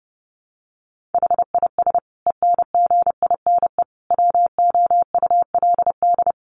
Morse code, heard as a keyed tone, about 30 wpm, 700 Hz. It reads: HIS ENGINE WOULD
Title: Sentences from Top 1500 Words 30wpm